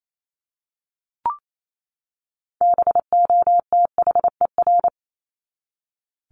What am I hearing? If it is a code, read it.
Morse code, heard as a keyed tone, about 28 wpm, 700 Hz. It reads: BOTHER
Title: Top 1501-1750 Words 28wpm